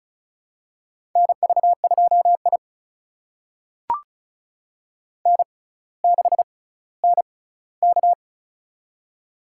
Morse code, heard as a keyed tone, about 36 wpm, spaced as for 18 wpm, 700 Hz. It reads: NV2I N6NK
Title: USA Call Signs - 4 Characters - 6x Character Spacing - 35wpm